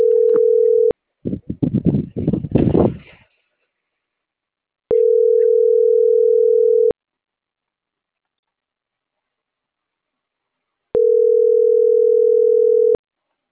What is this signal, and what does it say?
Ringback tone (North America), 440 + 480 Hz.